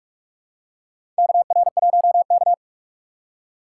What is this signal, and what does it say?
Morse code, heard as a keyed tone, about 46 wpm, 700 Hz. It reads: KR1K
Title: USA Call Signs - 4 Characters - 8x Character Spacing - 45wpm